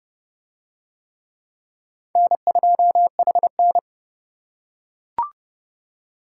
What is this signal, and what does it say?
Morse code, heard as a keyed tone, about 26 wpm, 700 Hz. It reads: N2HN